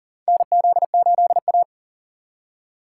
Morse code, heard as a keyed tone, about 40 wpm, 700 Hz. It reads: NZ8A
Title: USA Call Signs - 4 Characters - 2x Character Spacing - 40wpm